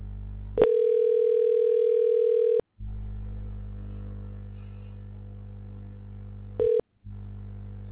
Ringback tone (North America), 440 + 480 Hz.